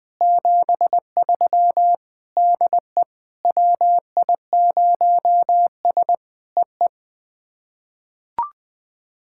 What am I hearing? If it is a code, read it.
Morse code, heard as a keyed tone, about 20 wpm, 700 Hz. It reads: 73 DE WI0S EE
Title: POTA QSO Practice - 20wpm